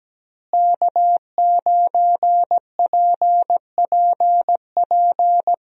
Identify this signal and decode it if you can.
Morse code, heard as a keyed tone, about 17 wpm, 700 Hz. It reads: K9PPP
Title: USA Call Signs - 5 Characters - Repeated 3x - 17wpm